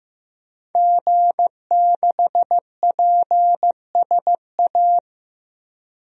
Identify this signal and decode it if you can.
Morse code, heard as a keyed tone, about 15 wpm, 700 Hz. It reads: G6PSA
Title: DXCC-100 Call Signs - 5 Characters - 15wpm